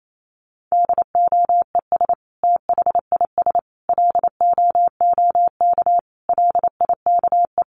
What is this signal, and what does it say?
Morse code, heard as a keyed tone, about 28 wpm, 700 Hz. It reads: DOES THIS LOOK LIKE
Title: Sentences from Top 1500 Words 28wpm